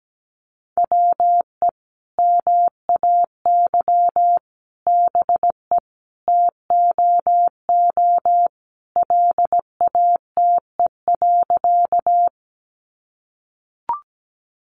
Morse code, heard as a keyed tone, about 17 wpm, 700 Hz. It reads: WE MAY BE TOO LATE.
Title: Sentences from Top 300 Words 17wpm